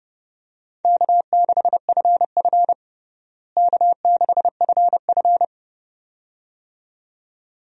Morse code, heard as a keyed tone, about 30 wpm, 700 Hz. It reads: K6FF K6FF
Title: USA Call Signs - 4 Characters - Repeated 2x - 30wpm